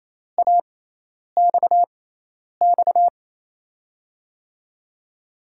Morse code, heard as a keyed tone, about 28 wpm, spaced as for 10 wpm, 700 Hz. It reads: AXX
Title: USA Call Signs - 6 Characters - 6x Character Spacing - 28wpm